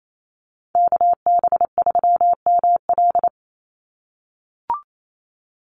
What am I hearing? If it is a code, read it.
Morse code, heard as a keyed tone, about 28 wpm, 700 Hz. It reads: KB3ML
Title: Call Signs - 28wpm